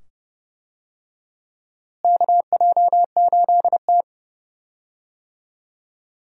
Morse code, heard as a keyed tone, about 30 wpm, 700 Hz. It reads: KJ8T
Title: USA Call Signs - 4 Characters - Repeated 2x - 30wpm